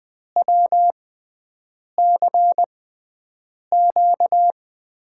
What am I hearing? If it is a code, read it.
Morse code, heard as a keyed tone, about 21 wpm, spaced as for 7 wpm, 700 Hz. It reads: WCQ
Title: USA Call Signs - 5 Characters - 6x Character Spacing - 20wpm